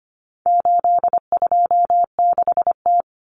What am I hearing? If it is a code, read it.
Morse code, heard as a keyed tone, about 25 wpm, 700 Hz. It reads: 826T
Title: Common Cut Numbers - 4-Digit 25wpm